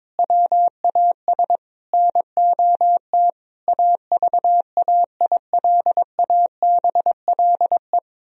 Morse code, heard as a keyed tone, about 22 wpm, 700 Hz. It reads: WAS NOT AVAILABLE
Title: Sentences from Top 1000 Words - Review Part 3 - 22wpm